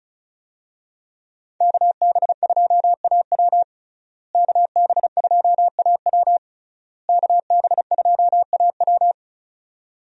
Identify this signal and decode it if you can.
Morse code, heard as a keyed tone, about 36 wpm, 700 Hz. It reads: KB2AW KB2AW KB2AW
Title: USA Call Signs - 5 Characters - Repeated 3x - 35wpm